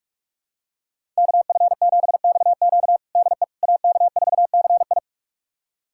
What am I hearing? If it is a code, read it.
Morse code, heard as a keyed tone, about 46 wpm, 700 Hz. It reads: KF7XQ DE AK4CI
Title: USA Callsign DE Callsign - 5 Characters - 45wpm